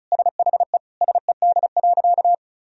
Morse code, heard as a keyed tone, about 34 wpm, 700 Hz. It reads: SHE SED.
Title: Sentences from Top 100 Words and Abbreviations 35wpm